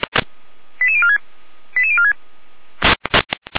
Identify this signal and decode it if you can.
Five-tone selective call (ZVEI), heard as quick runs of five tones, tones of about 70 ms. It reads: 88046 88046